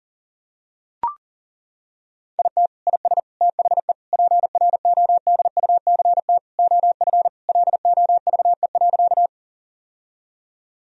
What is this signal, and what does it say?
Morse code, heard as a keyed tone, about 37 wpm, 700 Hz. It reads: IT IS THE PRODUCT OF LOVE.
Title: Sentences from Top 1000 Words - Review Part 2 - 40wpm